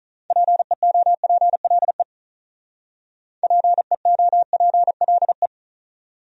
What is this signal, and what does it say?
Morse code, heard as a keyed tone, about 44 wpm, 700 Hz. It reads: PEOPLE PEOPLE